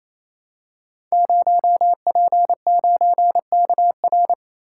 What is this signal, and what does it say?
Morse code, heard as a keyed tone, about 29 wpm, 700 Hz. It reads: 0P9KR
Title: Code Groups - 5 Letter:Numbers - 28wpm